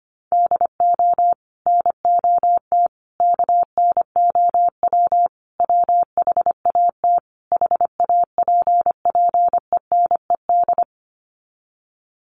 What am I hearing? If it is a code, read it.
Morse code, heard as a keyed tone, about 25 wpm, 700 Hz. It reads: DO NOT KNOW WHAT HAPPENED
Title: Sets of 5 Words Encore2 25wpm